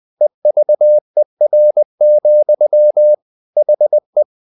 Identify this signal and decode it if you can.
Morse code, heard as a keyed tone, about 20 wpm, 600 Hz. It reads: EVER, HE
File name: the old man and the sea / the old man and the sea part 1 20wpm